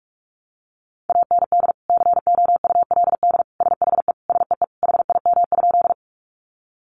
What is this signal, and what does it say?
Morse code, heard as a keyed tone, about 44 wpm, 700 Hz. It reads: AND COULD SHE SEE HIM?
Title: Sentences from Top 100 Words 45wpm